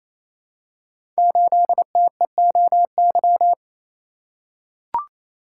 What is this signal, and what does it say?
Morse code, heard as a keyed tone, about 29 wpm, 700 Hz. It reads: 8TEOY
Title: Code Groups - 5 Letter:Numbers - 28wpm